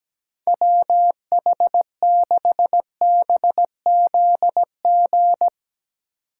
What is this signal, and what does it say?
Morse code, heard as a keyed tone, about 17 wpm, 700 Hz. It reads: WH6BZG
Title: Speed Racing - USA Call Signs 17wpm